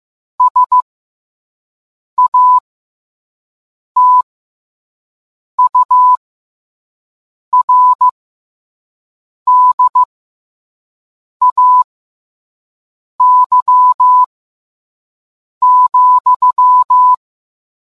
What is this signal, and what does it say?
Morse code, heard as a keyed tone, about 15 wpm, spaced as for 5 wpm, 1000 Hz. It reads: SATURDAY,